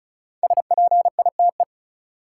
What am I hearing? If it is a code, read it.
Morse code, heard as a keyed tone, about 34 wpm, 700 Hz. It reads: SPITE
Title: Five Letter Words - 35wpm